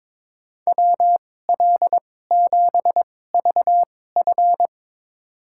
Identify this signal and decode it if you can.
Morse code, heard as a keyed tone, about 23 wpm, spaced as for 16 wpm, 700 Hz. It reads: WL7VF
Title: USA Call Signs - 5 Characters - 2x Character Spacing - 22wpm